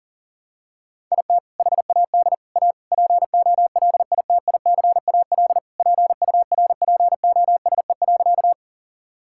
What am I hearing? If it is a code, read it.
Morse code, heard as a keyed tone, about 40 wpm, 700 Hz. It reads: IT HAD A POLITICAL PURPOSE.
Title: Sentences from Top 1000 Words - Review Part 5 - 40wpm